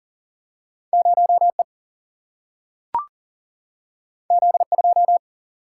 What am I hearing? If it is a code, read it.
Morse code, heard as a keyed tone, about 40 wpm, 700 Hz. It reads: 0E Z2
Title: Single-Letter Single-Number 40wpm